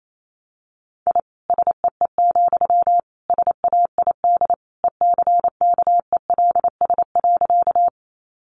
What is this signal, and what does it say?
Morse code, heard as a keyed tone, about 28 wpm, 700 Hz. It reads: I SEE, SAID ECKELS.
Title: A Sound Of Thunder 28wpm